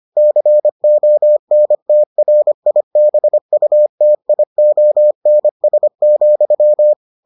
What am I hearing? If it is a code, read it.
Morse code, heard as a keyed tone, about 25 wpm, 600 Hz. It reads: CONTRIBUTIONS,